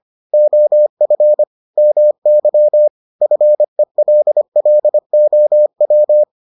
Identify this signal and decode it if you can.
Morse code, heard as a keyed tone, about 25 wpm, 600 Hz. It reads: OF MY FELLOW